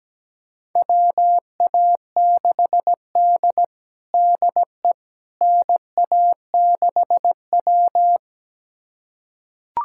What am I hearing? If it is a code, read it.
Morse code, heard as a keyed tone, about 17 wpm, 700 Hz. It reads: WA6D DE NA6W
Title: USA Callsign DE Callsign - 4 Characters - 17wpm